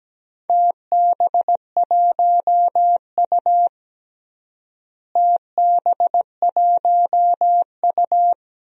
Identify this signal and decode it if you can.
Morse code, heard as a keyed tone, about 17 wpm, 700 Hz. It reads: TB1U TB1U
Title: DXCC-100 Call Signs - 4 Characters - Repeated 2x - 17wpm